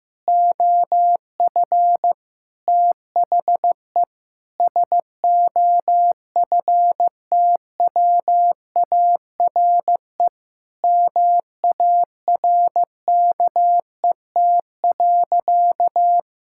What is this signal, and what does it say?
Morse code, heard as a keyed tone, about 16 wpm, 700 Hz. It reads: OF THE SOFTWARE MARKET.
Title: Sentences from Top 1000 Words - Review Part 6 - 15wpm